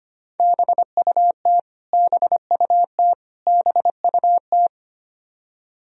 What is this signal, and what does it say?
Morse code, heard as a keyed tone, about 25 wpm, 700 Hz. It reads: BUT BUT BUT
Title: Top 100 Words Repeated 25wpm